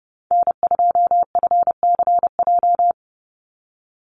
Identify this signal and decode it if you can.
Morse code, heard as a keyed tone, about 30 wpm, 700 Hz. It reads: N2FCJ
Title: USA Call Signs 30wpm